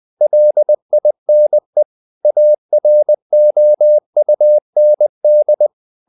Morse code, heard as a keyed tone, about 20 wpm, 600 Hz. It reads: LINE AROUND
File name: the old man and the sea / the old man and the sea part 1 20wpm